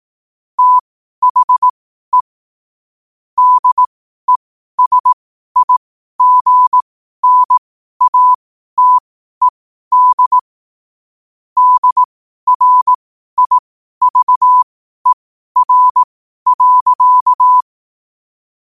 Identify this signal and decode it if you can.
Morse code, heard as a keyed tone, about 18 wpm, spaced as for 12 wpm, 1000 Hz. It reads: THE DESIGNATED DRIVER.
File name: qotdfiles / 12wpm